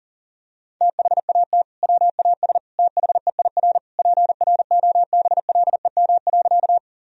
Morse code, heard as a keyed tone, about 37 wpm, 700 Hz. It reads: THAT WAS THEIR PROBLEM.
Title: Sentences from Top 1000 Words - Review Part 2 - 40wpm